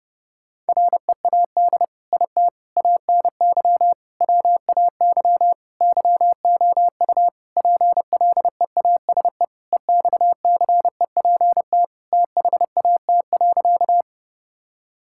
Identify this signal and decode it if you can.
Morse code, heard as a keyed tone, about 32 wpm, 700 Hz. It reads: READ IT ANY WAY YOU PLEASE EXCEPT THAT.
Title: Sentences from Top 1600 Words 30wpm